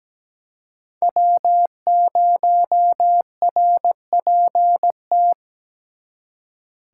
Morse code, heard as a keyed tone, about 20 wpm, 700 Hz. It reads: W0RPT
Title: Speed Racing - USA 1x3 Call Signs 17wpm